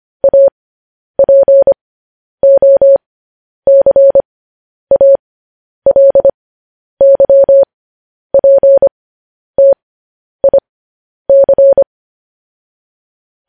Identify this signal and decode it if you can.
Morse code, heard as a keyed tone, about 25 wpm, spaced as for 10 wpm, 550 Hz. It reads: APOCALYPTIC
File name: podcast / News headlines-1x-25x10